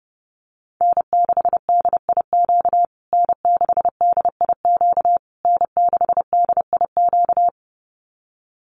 Morse code, heard as a keyed tone, about 28 wpm, 700 Hz. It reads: N6DIQ N6DIQ N6DIQ
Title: USA Call Signs Repeated 3x 30wpm